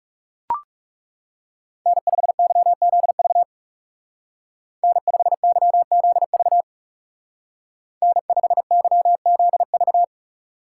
Morse code, heard as a keyed tone, about 34 wpm, 700 Hz. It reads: N5YZV N5YZV N5YZV
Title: Speed Racing - USA 1x3 Call Signs 30wpm